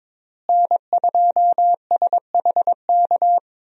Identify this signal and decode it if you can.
Morse code, heard as a keyed tone, about 22 wpm, 700 Hz. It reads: N2SHK
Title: USA Call Signs - 5 Characters - Repeated 3x - 22wpm